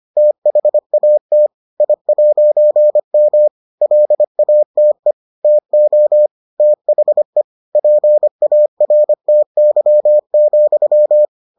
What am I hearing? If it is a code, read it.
Morse code, heard as a keyed tone, about 25 wpm, 600 Hz. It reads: THAT I'M LATE TO THE PARTY,